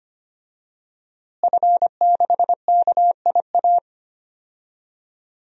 Morse code, heard as a keyed tone, about 25 wpm, 700 Hz. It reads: F6KIA
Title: DXCC-100 Call Signs - 5 Characters - Repeated 3x - 25wpm